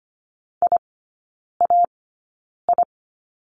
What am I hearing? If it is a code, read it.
Morse code, heard as a keyed tone, about 25 wpm, spaced as for 9 wpm, 690 Hz. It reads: IAI
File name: Lesson 15e - Number 5 - Three Characters 25wpm